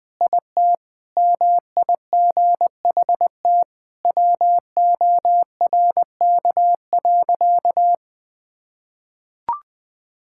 Morse code, heard as a keyed tone, about 21 wpm, 700 Hz. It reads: IT MIGHT WORK.
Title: Sentences with 3-Word Phrases - 20wpm